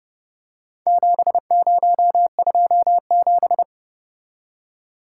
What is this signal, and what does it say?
Morse code, heard as a keyed tone, about 29 wpm, 700 Hz. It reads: 7027